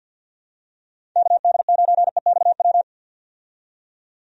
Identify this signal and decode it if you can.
Morse code, heard as a keyed tone, about 50 wpm, 700 Hz. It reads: KD9EXW
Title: USA Call Signs - 6 Characters - 50wpm